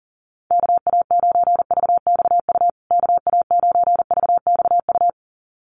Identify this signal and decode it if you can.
Morse code, heard as a keyed tone, about 40 wpm, 700 Hz. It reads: KA9VXU KA9VXU
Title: USA Call Signs Repeated 2x 40wpm